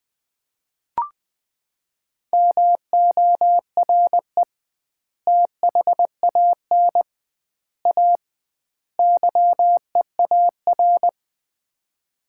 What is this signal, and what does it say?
Morse code, heard as a keyed tone, about 20 wpm, 700 Hz. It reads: MORE THAN A YEAR
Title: 2x Word-Spacing - Sets of 4 Words - 20wpm